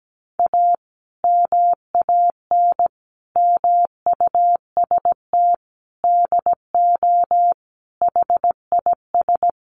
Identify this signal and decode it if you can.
Morse code, heard as a keyed tone, about 17 wpm, 700 Hz. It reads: A MAN MUST DO HIS
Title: Sentences from Top 1500 Words 17wpm